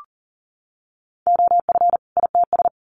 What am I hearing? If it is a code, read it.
Morse code, heard as a keyed tone, about 40 wpm, 700 Hz. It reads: OF ITS
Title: Sets of 2 Words 40wpm